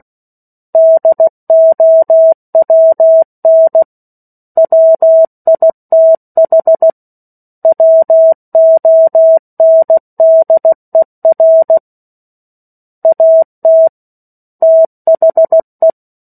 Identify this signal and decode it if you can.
Morse code, heard as a keyed tone, about 16 wpm, 650 Hz. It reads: DOWN WITH WONDER AT THE
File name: 03-Chapter-03 16 WPM